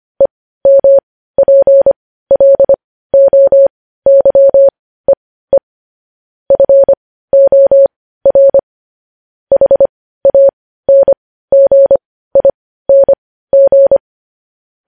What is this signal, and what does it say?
Morse code, heard as a keyed tone, about 25 wpm, spaced as for 15 wpm, 550 Hz. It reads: EMPLOYEE FOR HANGING
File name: podcast / News headlines-1x-25x15